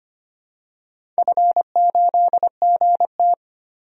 Morse code, heard as a keyed tone, about 25 wpm, 700 Hz. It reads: F8GT
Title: DXCC-100 Call Signs - 4 Characters - 25wpm